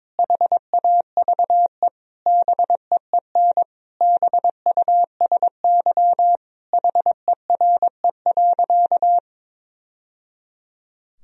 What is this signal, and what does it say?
Morse code, heard as a keyed tone, about 22 wpm, 700 Hz. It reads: HAVE BEEN BUSY HERE.
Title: Sentences from Top 1000 Words - Review Part 3 - 22wpm